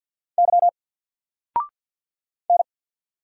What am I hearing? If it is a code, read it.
Morse code, heard as a keyed tone, about 50 wpm, 700 Hz. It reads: Y N
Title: Single Character (Letter:Number:Punctuation:Prosign) 50wpm